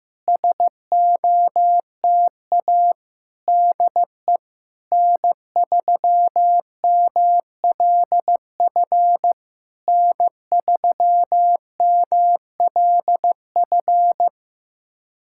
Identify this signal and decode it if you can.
Morse code, heard as a keyed tone, about 15 wpm, 700 Hz. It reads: SOTA DE N3MLF N3MLF
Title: SOTA QSO Practice 15wpm